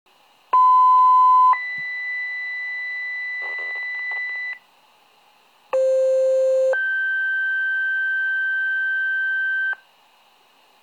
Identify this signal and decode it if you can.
Two-tone page, heard as a one-second tone, then a three-second tone, about 1010 then 2000 Hz, about 520 then 1550 Hz. It is heard from 0.5 to 9.8 s.
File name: pager alerting